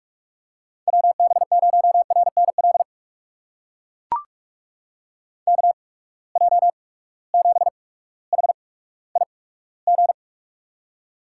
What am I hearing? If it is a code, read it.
Morse code, heard as a keyed tone, about 44 wpm, spaced as for 19 wpm, 700 Hz. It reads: WB0RNL KJ7HIG
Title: USA Call Signs - 6 Characters - 8x Character Spacing - 45wpm